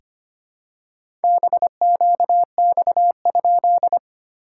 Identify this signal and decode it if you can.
Morse code, heard as a keyed tone, about 25 wpm, 690 Hz. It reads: BQX?
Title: SOTA QSO Practice 25wpm